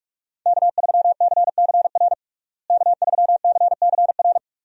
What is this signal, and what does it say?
Morse code, heard as a keyed tone, about 41 wpm, 700 Hz. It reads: K3CCR K3CCR
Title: USA Call Signs - 5 Characters - Repeated 3x - 45wpm